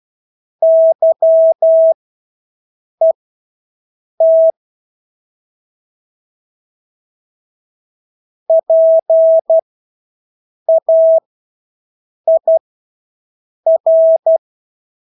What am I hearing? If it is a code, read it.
Morse code, heard as a keyed tone, about 12 wpm, spaced as for 6 wpm, 650 Hz. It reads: YET PAIR
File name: Engelske ord 12-6 wpm